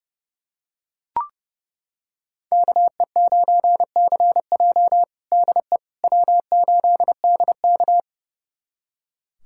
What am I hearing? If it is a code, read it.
Morse code, heard as a keyed tone, about 32 wpm, 700 Hz. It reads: KE9CJ DE W8DK
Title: USA Callsign DE Callsign 30wpm